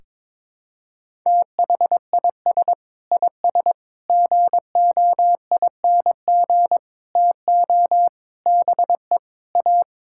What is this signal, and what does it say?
Morse code, heard as a keyed tone, about 22 wpm, 700 Hz. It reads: THIS IS GOING TO BE A
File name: practice / 22wpm / Sets of 6 Words - 22wpm